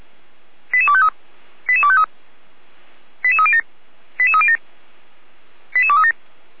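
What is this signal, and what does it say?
Five-tone selective call (ZVEI), heard as quick runs of five tones, tones of about 70 ms. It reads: 88253 88253 88307 88307 88237
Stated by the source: SDR recording of a ZVEI five-tone selective call, address 88209.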